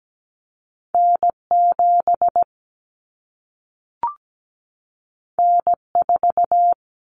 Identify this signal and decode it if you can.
Morse code, heard as a keyed tone, about 17 wpm, 700 Hz. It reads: N7 N4
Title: Common Cut Numbers - 2-Digit 17wpm